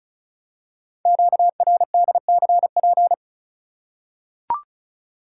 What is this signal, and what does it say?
Morse code, heard as a keyed tone, about 36 wpm, 700 Hz. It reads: QRDCP
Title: Code Groups - 5 Letter:Numbers - 35wpm